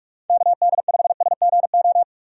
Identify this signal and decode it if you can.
Morse code, heard as a keyed tone, about 42 wpm, 700 Hz. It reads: KD5SGO
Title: USA Call Signs - 6 Characters - 45wpm